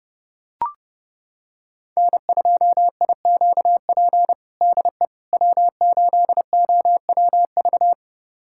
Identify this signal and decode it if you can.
Morse code, heard as a keyed tone, about 30 wpm, 700 Hz. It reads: N2IQP DE W8OWV
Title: USA Callsign DE Callsign - 5 Characters - 30wpm